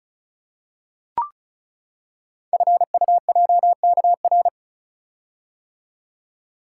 Morse code, heard as a keyed tone, about 36 wpm, 700 Hz. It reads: FUJKR
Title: Code Groups - 5 Letters - 35wpm